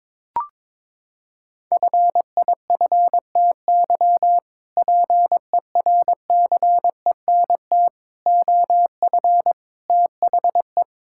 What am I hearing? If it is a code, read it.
Morse code, heard as a keyed tone, about 21 wpm, 700 Hz. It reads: FIFTY PERCENT OF THE
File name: practice / 22wpm / Sentences with 3-Word Phrases - 22wpm